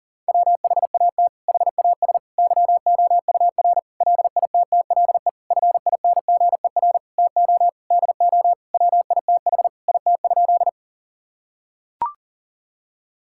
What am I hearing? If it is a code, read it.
Morse code, heard as a keyed tone, about 37 wpm, 700 Hz. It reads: WHAT HAS YOUR LITTLE FINGER TO DO WITH IT?
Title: Sentences from Top 1600 Words 40wpm